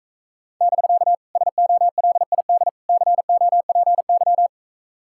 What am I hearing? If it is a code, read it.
Morse code, heard as a keyed tone, about 44 wpm, 700 Hz. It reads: <BK> SOLID COPY